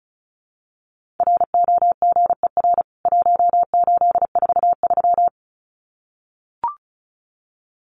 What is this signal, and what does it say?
Morse code, heard as a keyed tone, about 34 wpm, 700 Hz. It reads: ROGER 1843
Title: CWT Exchange 35wpm